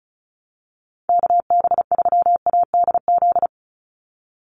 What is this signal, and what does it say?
Morse code, heard as a keyed tone, about 36 wpm, 700 Hz. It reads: KB3ADZ
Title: USA Call Signs 35wpm